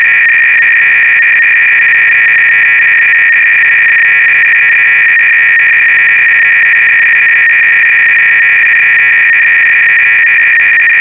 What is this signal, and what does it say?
300-baud modem data.
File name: ACR-Audio-Cassette-Programs / CharlesSwiger-Donations / KING cload-k